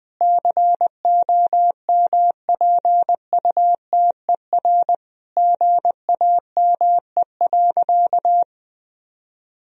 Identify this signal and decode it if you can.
Morse code, heard as a keyed tone, about 20 wpm, 700 Hz. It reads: COMPUTER GAME.
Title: Sentences from Top 1000 Words - Review Part 3 - 20wpm